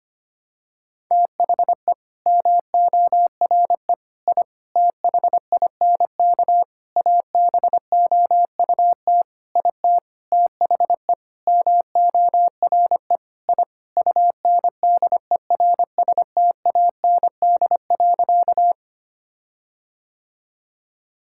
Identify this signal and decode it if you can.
Morse code, heard as a keyed tone, about 25 wpm, 700 Hz. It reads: THE MORE I THINK ABOUT IT THE MORE I UNDERSTAND.
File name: practice / 25wpm / Sentences with 9-Word Phrases - 25wpm